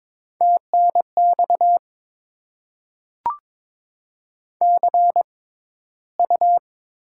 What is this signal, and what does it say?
Morse code, heard as a keyed tone, about 22 wpm, spaced as for 13 wpm, 700 Hz. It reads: TNX CU
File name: Top 20 QSO Elements - 6x Character Spacing - 22wpm